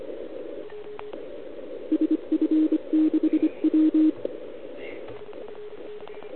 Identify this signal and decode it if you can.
Morse code, heard as a keyed tone, about 25 wpm, 310 Hz. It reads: SF6W
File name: ARCHIV / OKDX2020 / OL4Wa